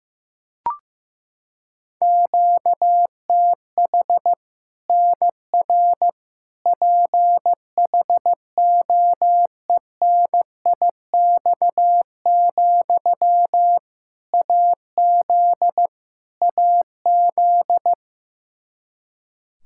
Morse code, heard as a keyed tone, about 15 wpm, 700 Hz. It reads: QTH NR PHOENIX, AZ AZ
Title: K1USN SST QSO Practice 15wpm